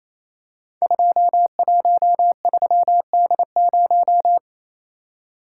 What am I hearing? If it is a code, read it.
Morse code, heard as a keyed tone, about 27 wpm, 700 Hz. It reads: 213D0
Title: Code Groups - 5 Letter:Numbers - 28wpm